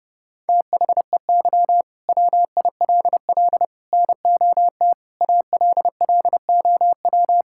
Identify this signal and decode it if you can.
Morse code, heard as a keyed tone, about 30 wpm, 700 Hz. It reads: THEY WILL NOT ALLOW
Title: Sentences from Top 1000 Words - Review Part 5 - 30wpm